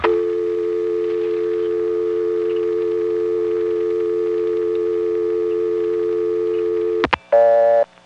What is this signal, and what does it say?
Dial tone (North America), 350 + 440 Hz.